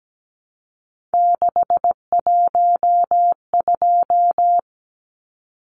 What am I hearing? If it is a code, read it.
Morse code, heard as a keyed tone, about 17 wpm, 700 Hz. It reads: 612
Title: Numbers - 3-Digit 17wpm